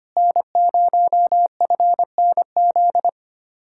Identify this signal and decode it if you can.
Morse code, heard as a keyed tone, about 24 wpm, 700 Hz. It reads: N0FNZ
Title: Speed-Racing Common QSO Phrases - 25wpm